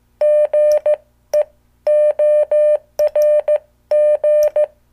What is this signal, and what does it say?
Morse code, heard as a keyed tone, about 15 wpm, spaced as for 13 wpm, 600 Hz. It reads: GEORG
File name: morse / qso